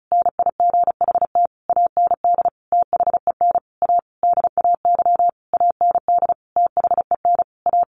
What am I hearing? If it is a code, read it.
Morse code, heard as a keyed tone, about 34 wpm, 700 Hz. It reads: NIGHT AND THEN A DAY AND THEN A
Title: A Sound Of Thunder 35wpm